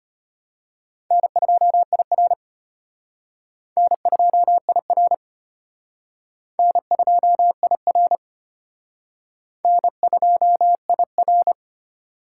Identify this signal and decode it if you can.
Morse code, heard as a keyed tone, about 32 wpm, 700 Hz. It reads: N2IR N2IR N2IR N2IR